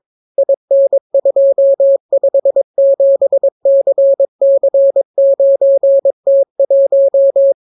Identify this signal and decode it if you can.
Morse code, heard as a keyed tone, about 22 wpm, 550 Hz. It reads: IN257CC9T1